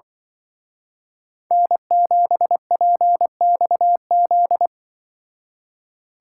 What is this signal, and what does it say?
Morse code, heard as a keyed tone, about 24 wpm, 700 Hz. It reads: N7PXZ
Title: Speed Racing - USA Call Signs 20wpm